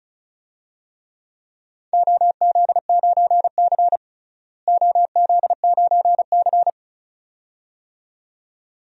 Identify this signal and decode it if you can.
Morse code, heard as a keyed tone, about 36 wpm, 700 Hz. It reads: OZ9C OZ9C
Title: DXCC-100 Call Signs - 4 Characters - Repeated 2x - 35wpm